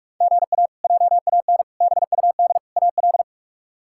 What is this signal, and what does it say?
Morse code, heard as a keyed tone, about 41 wpm, 700 Hz. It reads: GA JAN BUD AL
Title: K1USN SST QSO Practice - 45wpm